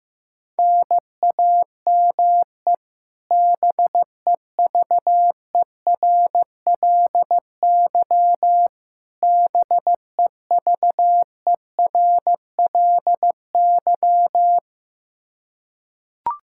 Morse code, heard as a keyed tone, about 15 wpm, 700 Hz. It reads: NAME BEVERLY BEVERLY
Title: Practice QSO - Exchange 1 - Beginning - 15wpm